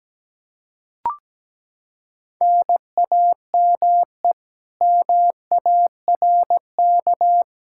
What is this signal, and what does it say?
Morse code, heard as a keyed tone, about 17 wpm, 700 Hz. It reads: NAME MARK
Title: Practice QSO - Exchange 1 - Beginning - 17wpm